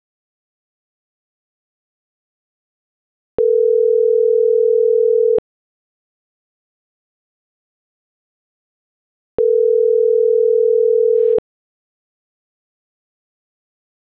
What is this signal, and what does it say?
Ringback tone (North America), 440 + 480 Hz.